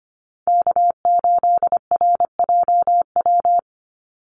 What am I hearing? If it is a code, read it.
Morse code, heard as a keyed tone, about 25 wpm, 700 Hz. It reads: K8RJW
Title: Call Signs - 25wpm